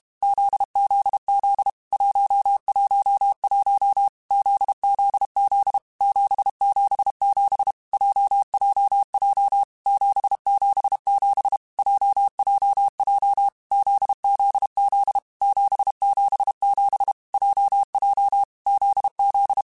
Morse code, heard as a keyed tone, about 32 wpm, 800 Hz. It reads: ZZZ 111 ZZZ 777 JJJ 777 JJJ ZZZ 777 JJ ZZ
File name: Basic Letter Groups By7Groups / JZ17 / JZ17@30